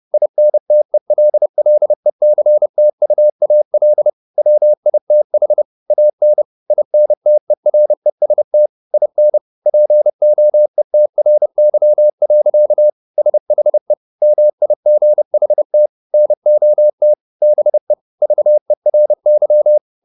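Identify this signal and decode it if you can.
Morse code, heard as a keyed tone, about 30 wpm, 600 Hz. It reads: INTELLECTUAL WITH AN INTEREST IN POETRY. SHE MIGHT NOT BE VERY
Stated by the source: Morse code at 30 wpm, the novel Never by Ken Follett in English, from CW med Gnister.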